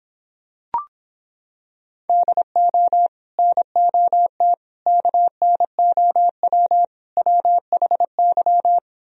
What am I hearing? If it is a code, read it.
Morse code, heard as a keyed tone, about 26 wpm, 700 Hz. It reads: DO NOT KNOW WHY